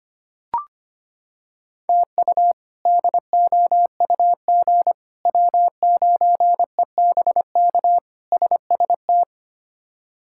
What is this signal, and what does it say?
Morse code, heard as a keyed tone, about 25 wpm, 700 Hz. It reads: TU DOUG W9EBK SST
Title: K1USN SST QSO Practice - 25wpm